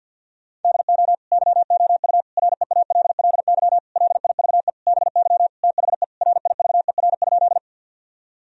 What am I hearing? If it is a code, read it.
Morse code, heard as a keyed tone, about 51 wpm, 700 Hz. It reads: DO YOU REALLY LIVE BY THE RIVER?
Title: Sentences from Top 1000 Words - Review Part 1 - 50wpm